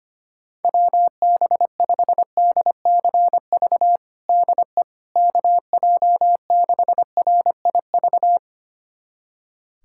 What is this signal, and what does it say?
Morse code, heard as a keyed tone, about 25 wpm, 700 Hz. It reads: WB5DCV DE KJ6RIV
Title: USA Callsign DE Callsign - 6 Characters - 25wpm